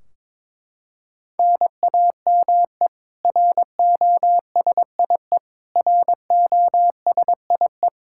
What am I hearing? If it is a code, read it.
Morse code, heard as a keyed tone, about 22 wpm, 700 Hz. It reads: NAME ROSIE ROSIE
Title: K1USN SST QSO Practice 22wpm